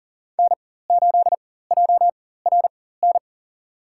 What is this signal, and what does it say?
Morse code, heard as a keyed tone, about 37 wpm, spaced as for 18 wpm, 700 Hz. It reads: N8JRN
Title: USA Call Signs - 5 Characters - 4x Character Spacing - 40wpm